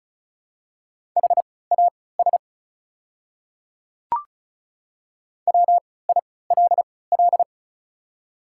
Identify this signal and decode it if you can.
Morse code, heard as a keyed tone, about 34 wpm, spaced as for 20 wpm, 690 Hz. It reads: HAS WILL
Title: 3x Character Spacing - Top 100 Words - 35wpm